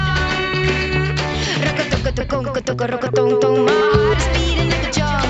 Ringback tone (Europe, 425 Hz), with noise.